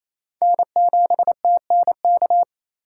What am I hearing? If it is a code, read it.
Morse code, heard as a keyed tone, about 28 wpm, 700 Hz. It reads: N7TNK
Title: USA Call Signs - 5 Characters - 2x Character Spacing - 28wpm